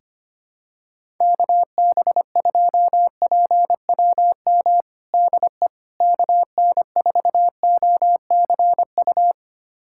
Morse code, heard as a keyed tone, about 25 wpm, 700 Hz. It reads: KB2PWM DE KN4OCU
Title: USA Callsign DE Callsign - 6 Characters - 25wpm